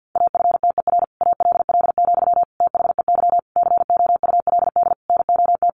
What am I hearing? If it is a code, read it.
Morse code, heard as a keyed tone, about 50 wpm, 700 Hz. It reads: AFTER ALL, THEY COULD NOT
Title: Sentences from Top 1100 Words 50wpm